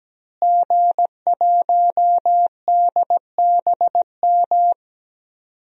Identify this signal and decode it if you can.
Morse code, heard as a keyed tone, about 17 wpm, 700 Hz. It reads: G1DBM
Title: DXCC-100 Call Signs - 5 Characters - Repeated 3x - 17wpm